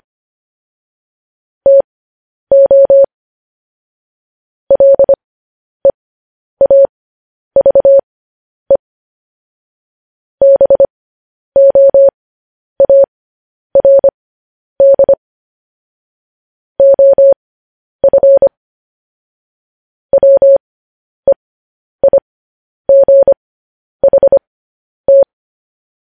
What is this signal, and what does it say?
Morse code, heard as a keyed tone, about 25 wpm, spaced as for 10 wpm, 550 Hz. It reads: TO LEAVE BOARD OF WEIGHT